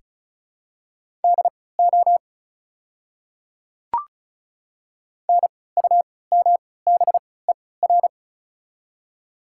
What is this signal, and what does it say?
Morse code, heard as a keyed tone, about 36 wpm, spaced as for 20 wpm, 690 Hz. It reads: DO NUMBER
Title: 3x Character Spacing - Top 100 Words - 35wpm